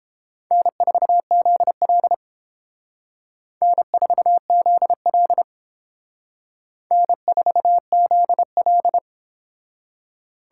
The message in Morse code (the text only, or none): N4ZL N4ZL N4ZL